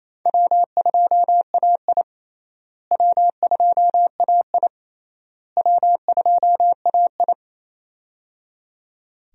Morse code, heard as a keyed tone, about 28 wpm, 700 Hz. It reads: W2AI W2AI W2AI
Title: USA Call Signs - 4 Characters - Repeated 3x - 28wpm